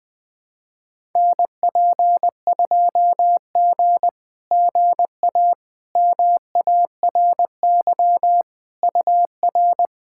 Morse code, heard as a keyed tone, about 20 wpm, 690 Hz. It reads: NP2G GA MARY UR
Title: SOTA QSO Practice 20wpm